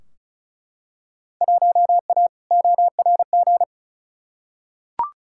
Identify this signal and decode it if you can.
Morse code, heard as a keyed tone, about 34 wpm, 700 Hz. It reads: 1A ORG
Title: Field Day Exchanges Encore - 35wpm